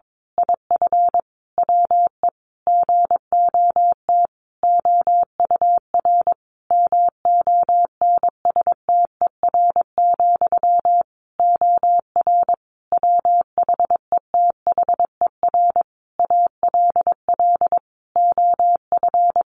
Morse code, heard as a keyed tone, about 22 wpm, 700 Hz. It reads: IF WE GOT OUR MONSTER, OR WHETHER ALL OF
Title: A Sound Of Thunder 22wpm